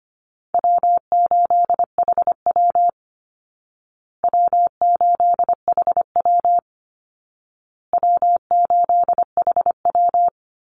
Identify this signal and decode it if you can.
Morse code, heard as a keyed tone, about 25 wpm, 700 Hz. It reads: W8HW W8HW W8HW
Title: Easy Call Signs Repeated 3x - 25wpm